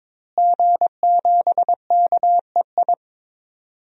Morse code, heard as a keyed tone, about 22 wpm, 700 Hz. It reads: G7KEI
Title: DXCC-100 Call Signs - 5 Characters - Repeated 3x - 22wpm